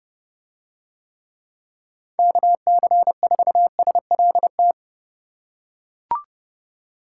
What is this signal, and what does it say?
Morse code, heard as a keyed tone, about 29 wpm, 700 Hz. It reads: KC4SLT